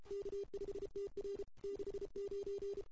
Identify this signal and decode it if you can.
Morse code, heard as a keyed tone, about 31 wpm, 400 Hz. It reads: K5TR 69
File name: CW / 2008-ss-cw-k5tr-op-n5rz -02:00